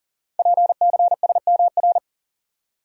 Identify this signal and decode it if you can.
Morse code, heard as a keyed tone, about 38 wpm, 690 Hz. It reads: PCSMR
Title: Code Groups - 5 Letter:Numbers - 40wpm